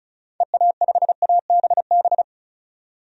Morse code, heard as a keyed tone, about 36 wpm, 700 Hz. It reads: EA5ABB
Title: DXCC-100 Call Signs - 6 Characters - 35wpm